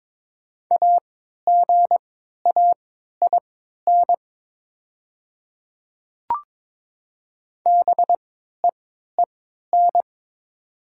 Morse code, heard as a keyed tone, about 22 wpm, spaced as for 12 wpm, 700 Hz. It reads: AGAIN BEEN
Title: 3x Character Spacing - Top 200 Words - 22wpm